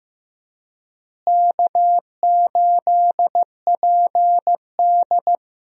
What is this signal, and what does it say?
Morse code, heard as a keyed tone, about 15 wpm, 700 Hz. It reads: K8PD
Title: USA Call Signs - 4 Characters - Repeated 3x - 15wpm